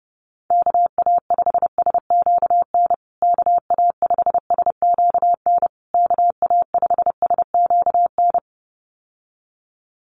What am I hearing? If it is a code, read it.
Morse code, heard as a keyed tone, about 30 wpm, 700 Hz. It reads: KA5SQN KA5SQN KA5SQN
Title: USA Call Signs Repeated 3x 30wpm